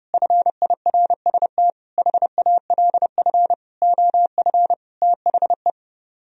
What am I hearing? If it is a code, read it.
Morse code, heard as a keyed tone, about 30 wpm, 700 Hz. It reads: FIRST HALF OF THE